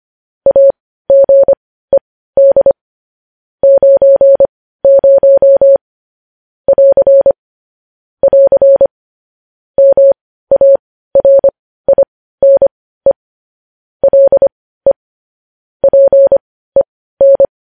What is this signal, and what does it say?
Morse code, heard as a keyed tone, about 25 wpm, spaced as for 15 wpm, 550 Hz. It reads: AGED 90 <AR> <AR> MARINE LE PEN
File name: podcast / News headlines-1x-25x15